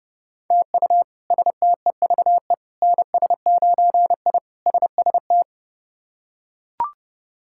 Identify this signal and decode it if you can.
Morse code, heard as a keyed tone, about 32 wpm, 700 Hz. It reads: TU STEVE NS9I SST
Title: K1USN SST QSO Practice - 30wpm